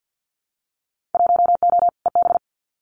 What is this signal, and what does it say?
Morse code, heard as a keyed tone, about 50 wpm, 700 Hz. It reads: 1O EB
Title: Winter Field Day Exchanges 50wpm